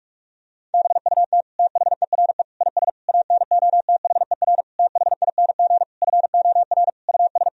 Morse code, heard as a keyed tone, about 43 wpm, 700 Hz. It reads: BUT THERE IS ANOTHER THING FOR US
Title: Sentences from Top 1000 Words - Review Part 4 - 45wpm